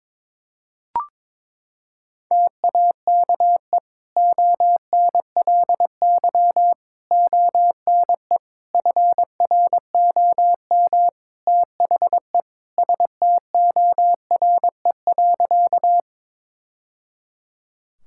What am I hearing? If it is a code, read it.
Morse code, heard as a keyed tone, about 22 wpm, 700 Hz. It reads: TAKE ONLY ONE FROM THE STORE.
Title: Sentences from Top 1000 Words - Review Part 5 - 22wpm